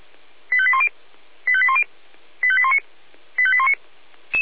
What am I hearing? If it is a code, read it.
Five-tone selective call (ZVEI), heard as quick runs of five tones, tones of about 80 ms. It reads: 85810 85810 85819 85819
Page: SDR/ZVEI